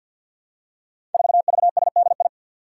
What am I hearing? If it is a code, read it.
Morse code, heard as a keyed tone, about 47 wpm, 700 Hz. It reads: 4VSDI
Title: Five Challenging Letter-Number Combinations 50wpm